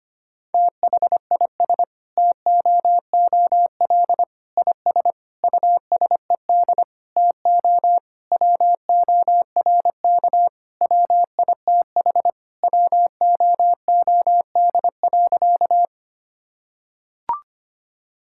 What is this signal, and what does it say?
Morse code, heard as a keyed tone, about 25 wpm, 700 Hz. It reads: THIS TOOL IS USED TO WORK WITH WOOD.
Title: Sentences from Top 1000 Words - Review Part 6 - 25wpm